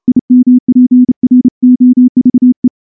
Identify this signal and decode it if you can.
Morse code, heard as a keyed tone, about 29 wpm, 250 Hz. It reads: IMPROVE
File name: morse-training-30WPM